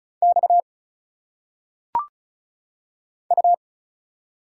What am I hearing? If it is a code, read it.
Morse code, heard as a keyed tone, about 33 wpm, 700 Hz. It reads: X U
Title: Single Letters - Phonetic Pronunciation 35wpm